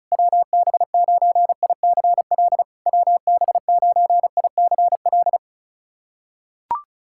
Morse code, heard as a keyed tone, about 34 wpm, 690 Hz. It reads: WB9ICL WB9ICL
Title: SOTA QSO Practice 35wpm